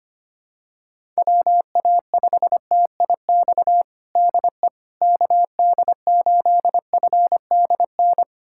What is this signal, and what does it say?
Morse code, heard as a keyed tone, about 25 wpm, 700 Hz. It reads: WA5TIX DE KD8FDN
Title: USA Callsign DE Callsign - 6 Characters - 25wpm